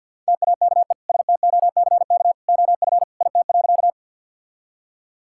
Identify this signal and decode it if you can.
Morse code, heard as a keyed tone, about 50 wpm, 700 Hz. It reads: TAKE STOCK OF IT.
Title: Sentences from Top 1000 Words - Review Part 5 - 50wpm